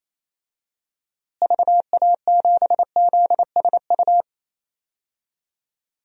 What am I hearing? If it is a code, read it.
Morse code, heard as a keyed tone, about 28 wpm, 700 Hz. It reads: VA7ZSU
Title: DXCC-100 Call Signs - 6 Characters - Repeated 2x - 28wpm